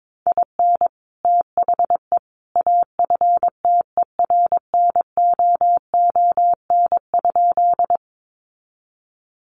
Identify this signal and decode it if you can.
Morse code, heard as a keyed tone, about 22 wpm, 700 Hz. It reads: IN THE AFTERNOON?
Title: Sentences from Top 1500 Words 22wpm